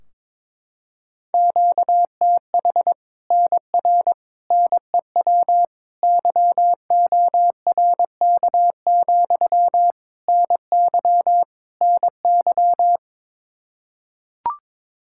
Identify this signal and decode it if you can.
Morse code, heard as a keyed tone, about 22 wpm, 700 Hz. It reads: QTH NR NEW YORK, NY NY
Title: Practice QSO - Exchange 1 - Beginning - 22wpm